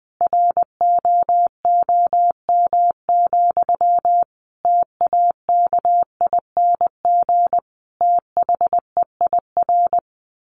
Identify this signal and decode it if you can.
Morse code, heard as a keyed tone, about 20 wpm, 700 Hz. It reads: ROOM, TAKING THEIR
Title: A Sound Of Thunder 20wpm